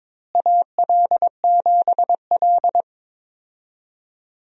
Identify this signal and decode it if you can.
Morse code, heard as a keyed tone, about 22 wpm, 700 Hz. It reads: AL7L
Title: USA Call Signs - 4 Characters - Repeated 3x - 22wpm